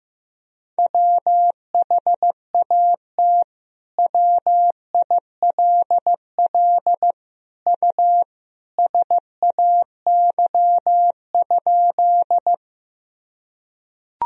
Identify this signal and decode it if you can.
Morse code, heard as a keyed tone, about 16 wpm, 700 Hz. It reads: WHAT WILL U SAY?
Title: Sentences from Top 200 Words and Abbreviations 15wpm